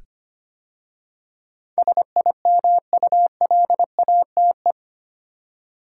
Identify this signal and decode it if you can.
Morse code, heard as a keyed tone, about 25 wpm, 700 Hz. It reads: SIMULATE
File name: 8 Letter Words - First 3 Letters Given - 25wpm